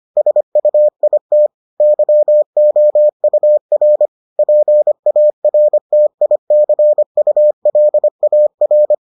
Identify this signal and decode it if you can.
Morse code, heard as a keyed tone, about 25 wpm, 600 Hz. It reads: SUIT YOUR PARTICULAR